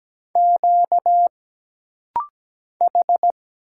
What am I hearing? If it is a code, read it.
Morse code, heard as a keyed tone, about 17 wpm, 700 Hz. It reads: Q H
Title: Single Letter-Number Alt - Rapid-Fire 17wpm